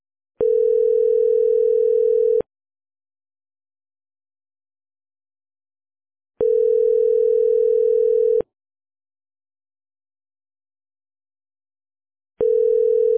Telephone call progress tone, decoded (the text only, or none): ringback tone